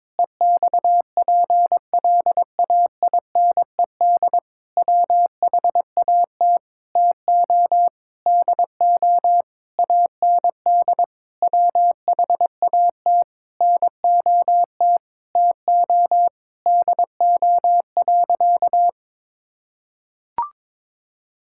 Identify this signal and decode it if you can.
Morse code, heard as a keyed tone, about 22 wpm, 700 Hz. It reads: EXPLAINED WHAT TO DO AND WHAT NOT TO DO.
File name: practice / 22wpm / Sentences with 8-Word Phrases - 22wpm